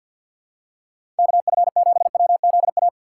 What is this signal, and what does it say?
Morse code, heard as a keyed tone, about 53 wpm, 700 Hz. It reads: KF7WZA
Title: USA Call Signs - 6 Characters - 6x Character Spacing - 50wpm